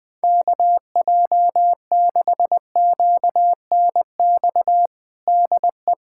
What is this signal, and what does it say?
Morse code, heard as a keyed tone, about 20 wpm, 700 Hz. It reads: KJ6QNX DE
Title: USA Callsign DE Callsign - 6 Characters - 20wpm